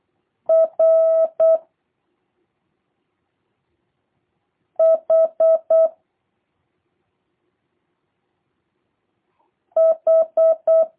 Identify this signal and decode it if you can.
Morse code, heard as a keyed tone, about 8 wpm, 650 Hz. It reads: R H H
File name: jh8cbh / 2023cw / review1-2